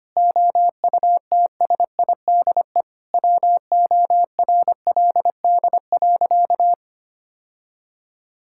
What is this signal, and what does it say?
Morse code, heard as a keyed tone, about 25 wpm, 700 Hz. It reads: OUTSIDE WORLD.
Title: Sentences from Top 1000 Words - Review Part 6 - 25wpm